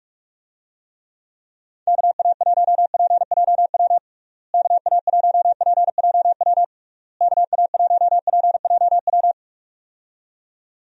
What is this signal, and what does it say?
Morse code, heard as a keyed tone, about 46 wpm, 700 Hz. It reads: KA1PJW KA1PJW KA1PJW
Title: USA Call Signs - 6 Characters - Repeated 3x - 45wpm